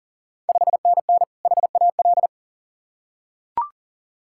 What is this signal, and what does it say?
Morse code, heard as a keyed tone, about 37 wpm, 700 Hz. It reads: 5NN HAL
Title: Ontario QSO Party Exchange 40wpm